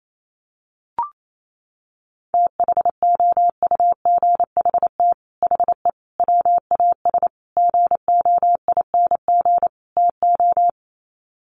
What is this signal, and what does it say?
Morse code, heard as a keyed tone, about 28 wpm, 700 Hz. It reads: THOUGHT HE WAS GOING TO
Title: Sets of 5 Words Encore2 28wpm